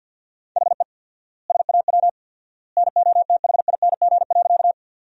Morse code, heard as a keyed tone, about 50 wpm, 700 Hz. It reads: HE SAW NOTHING.
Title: 4x Word-Spacing - Sentences from Top 500 Words - 50wpm